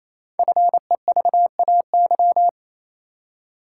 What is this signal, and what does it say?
Morse code, heard as a keyed tone, about 27 wpm, 700 Hz. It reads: FEVAY
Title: Code Groups - 5 Letter:Numbers - 28wpm